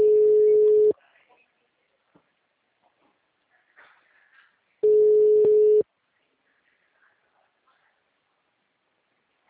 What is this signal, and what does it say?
Ringback tone (Europe, 425 Hz), with speech.